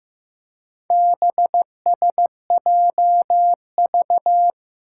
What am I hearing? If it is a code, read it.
Morse code, heard as a keyed tone, about 15 wpm, 700 Hz. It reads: BSJV
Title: Four Challenging Letter-Number Combinations 15wpm